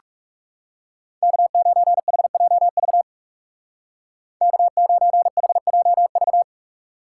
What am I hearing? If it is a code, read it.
Morse code, heard as a keyed tone, about 36 wpm, 700 Hz. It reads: K9HJV K9HJV
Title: Speed Racing - USA 1x3 Call Signs 30wpm